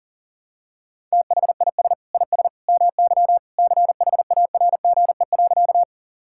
Morse code, heard as a keyed tone, about 40 wpm, 700 Hz. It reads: THIS IS MY CHARGE.
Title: Sentences from Top 1000 Words - Review Part 6 - 40wpm